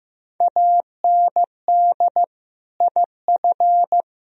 Morse code, heard as a keyed tone, about 15 wpm, 700 Hz. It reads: AND IF
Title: Speed-Racing - Sets of 5 Words 15wpm